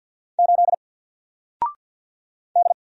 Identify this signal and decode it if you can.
Morse code, heard as a keyed tone, about 48 wpm, 700 Hz. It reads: 8 D
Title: Single Letter-Number Alt 50wpm